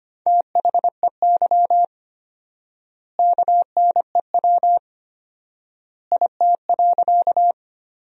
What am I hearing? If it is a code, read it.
Morse code, heard as a keyed tone, about 25 wpm, 700 Hz. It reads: THEY KNEW IT.
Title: 4x Word-Spacing - Sentences from Top 400 Words - 25wpm